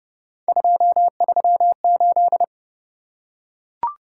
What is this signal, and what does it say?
Morse code, heard as a keyed tone, about 27 wpm, 700 Hz. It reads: 238